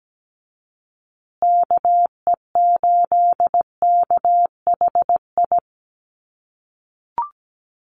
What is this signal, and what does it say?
Morse code, heard as a keyed tone, about 17 wpm, 700 Hz. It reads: KE8KHI
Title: USA Call Signs Repeated 2x 17wpm